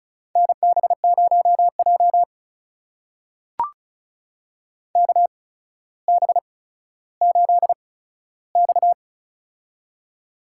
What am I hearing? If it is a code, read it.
Morse code, heard as a keyed tone, about 36 wpm, spaced as for 15 wpm, 700 Hz. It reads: NB0J KB8X
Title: USA Call Signs - 8x Character Spacing - 35wpm